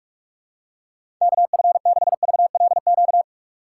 Morse code, heard as a keyed tone, about 45 wpm, 700 Hz. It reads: KF6VLQ
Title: USA Call Signs - 6 Characters - 2x Character Spacing - 45wpm